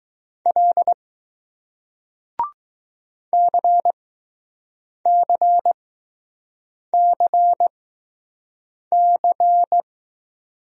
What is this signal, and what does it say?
Morse code, heard as a keyed tone, about 20 wpm, spaced as for 6 wpm, 700 Hz. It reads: L CCCC